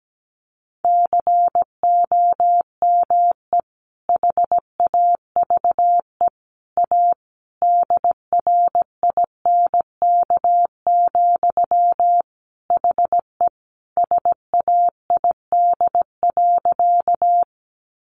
Morse code, heard as a keyed tone, about 17 wpm, 700 Hz. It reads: COME HAVE A DRINK, HE SAID.
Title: Sentences from Top 1500 Words 17wpm